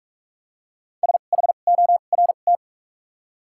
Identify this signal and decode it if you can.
Morse code, heard as a keyed tone, about 44 wpm, spaced as for 32 wpm, 700 Hz. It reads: SHORT
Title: 2x Character Spacing - Top 500 Words - 45wpm